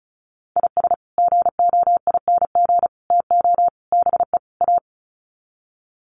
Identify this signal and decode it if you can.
Morse code, heard as a keyed tone, about 34 wpm, 700 Hz. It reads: IS GOING TO BE A
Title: Sets of 5 Words 35wpm